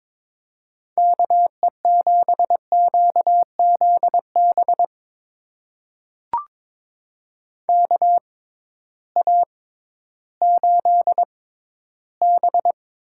Morse code, heard as a keyed tone, about 22 wpm, spaced as for 13 wpm, 700 Hz. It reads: KE7QZB KA8B
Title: USA Call Signs - 6 Characters - 6x Character Spacing - 22wpm